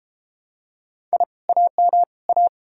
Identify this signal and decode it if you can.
Morse code, heard as a keyed tone, about 32 wpm, 700 Hz. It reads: I AM A